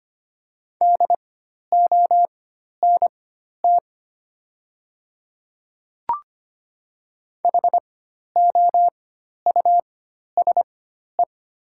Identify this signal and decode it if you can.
Morse code, heard as a keyed tone, about 25 wpm, spaced as for 12 wpm, 700 Hz. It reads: DONT HOUSE
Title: 4x Character Spacing - Top 500 Words - 25wpm